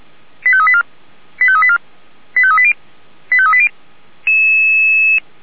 Five-tone selective call (ZVEI), heard as quick runs of five tones, tones of about 70 ms. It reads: 85384 85384 85380 85380